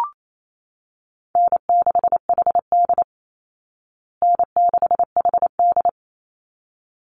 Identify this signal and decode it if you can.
Morse code, heard as a keyed tone, about 28 wpm, 690 Hz. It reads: N6HD N6HD
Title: Call Signs Repeated 3x - 28wpm